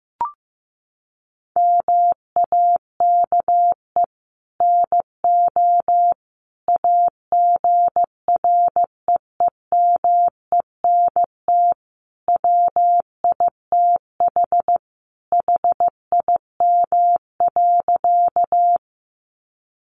Morse code, heard as a keyed tone, about 15 wpm, 700 Hz. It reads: MAKE NO AGREEMENT WITH HIM.
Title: Sentences from Top 1500 Words 15wpm